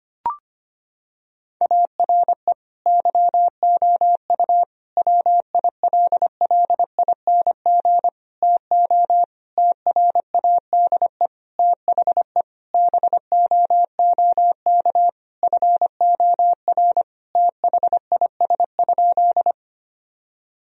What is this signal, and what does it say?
Morse code, heard as a keyed tone, about 25 wpm, 700 Hz. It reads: ARE YOU WILLING TO TRADE THE BOOK FOR THIS?
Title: Sentences from Top 1000 Words - Review Part 4 - 25wpm